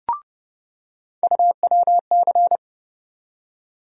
Morse code, heard as a keyed tone, about 30 wpm, 700 Hz. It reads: UWC
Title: USA Call Sign Postfixes - 3 Letters 30wpm